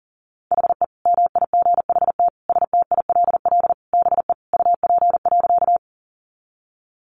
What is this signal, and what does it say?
Morse code, heard as a keyed tone, about 38 wpm, 700 Hz. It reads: HE MIGHT STILL BE UP.
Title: Sentences from Top 300 Words 40wpm